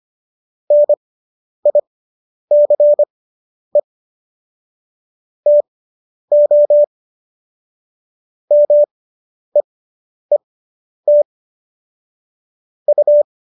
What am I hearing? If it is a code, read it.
Morse code, heard as a keyed tone, about 25 wpm, spaced as for 10 wpm, 590 Hz. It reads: NICE TO MEET U